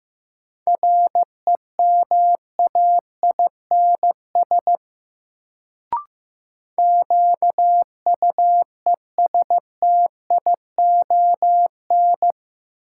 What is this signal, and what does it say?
Morse code, heard as a keyed tone, about 15 wpm, 700 Hz. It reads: REMAINS QUESTION
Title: Sets of 5 Words Encore - New Word Drill - 15wpm